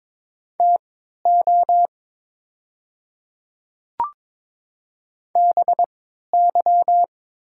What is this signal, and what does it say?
Morse code, heard as a keyed tone, about 22 wpm, spaced as for 13 wpm, 700 Hz. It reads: TO BY
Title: 3x Character Spacing - Top 100 Words - 22wpm